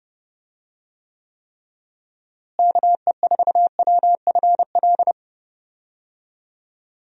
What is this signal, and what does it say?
Morse code, heard as a keyed tone, about 30 wpm, 700 Hz. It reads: KE4WFL
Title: USA Call Signs - 6 Characters - Repeated 3x - 30wpm